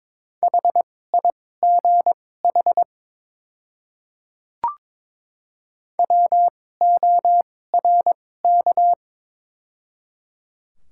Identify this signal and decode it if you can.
Morse code, heard as a keyed tone, about 22 wpm, spaced as for 16 wpm, 700 Hz. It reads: HIGH WORK
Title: 2x Character Spacing - Top 200 Words - 22wpm